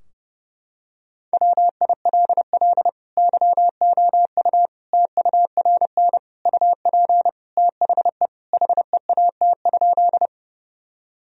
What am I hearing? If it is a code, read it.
Morse code, heard as a keyed tone, about 29 wpm, 700 Hz. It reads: WILL YOU TURN UP THE HEAT?
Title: Sentences from Top 1000 Words - Review Part 3 - 30wpm